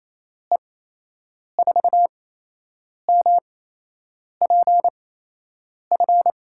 Morse code, heard as a keyed tone, about 28 wpm, spaced as for 8 wpm, 700 Hz. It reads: E4MPF
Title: USA Call Signs - 6 Characters - 8x Character Spacing - 28wpm